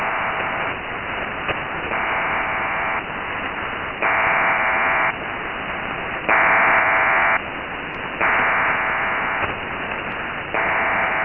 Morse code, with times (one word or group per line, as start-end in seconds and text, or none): none